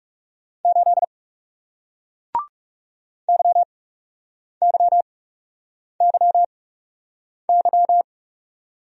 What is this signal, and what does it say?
Morse code, heard as a keyed tone, about 40 wpm, spaced as for 8 wpm, 700 Hz. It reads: 8 YYYY